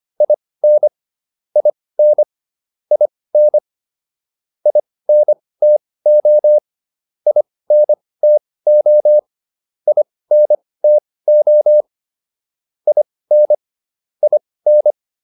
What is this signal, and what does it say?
Morse code, heard as a keyed tone, about 25 wpm, spaced as for 18 wpm, 600 Hz. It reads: IN IN IN INTO INTO INTO IN IN